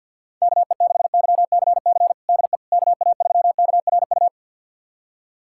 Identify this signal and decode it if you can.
Morse code, heard as a keyed tone, about 52 wpm, 700 Hz. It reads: KE6YCC DE KA3KRU
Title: USA Callsign DE Callsign 50wpm